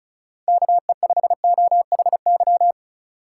Morse code, heard as a keyed tone, about 36 wpm, 700 Hz. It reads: KE5OHY
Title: USA Call Signs - 6 Characters - 8x Character Spacing - 35wpm